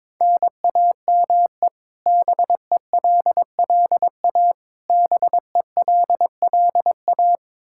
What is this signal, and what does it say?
Morse code, heard as a keyed tone, about 22 wpm, 700 Hz. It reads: NAME BELLA BELLA
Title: Practice QSO - Exchange 1 - Beginning - 22wpm